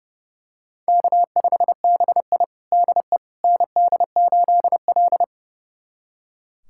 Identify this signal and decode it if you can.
Morse code, heard as a keyed tone, about 32 wpm, 700 Hz. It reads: K5BI DE ND8L
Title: USA Callsign DE Callsign - 4 Characters - 30wpm